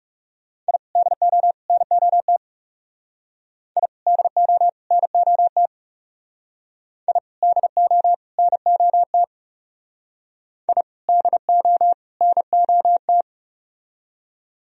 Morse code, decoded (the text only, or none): I DO NOT I DO NOT I DO NOT I DO NOT